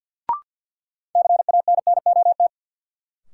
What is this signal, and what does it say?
Morse code, heard as a keyed tone, about 53 wpm, 700 Hz. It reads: CANNOT
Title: Sentences from Top 1000 Words - Review Part 6 - New Word Drill - 50wpm